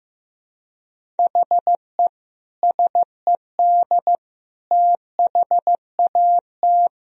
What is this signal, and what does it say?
Morse code, heard as a keyed tone, about 15 wpm, 700 Hz. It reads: HE SED THAT
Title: Sentences from Top 200 Words and Abbreviations 15wpm